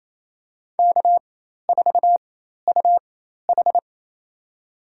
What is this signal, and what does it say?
Morse code, heard as a keyed tone, about 27 wpm, spaced as for 13 wpm, 700 Hz. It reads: K4UH
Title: USA Call Signs - 4 Characters - 4x Character Spacing - 28wpm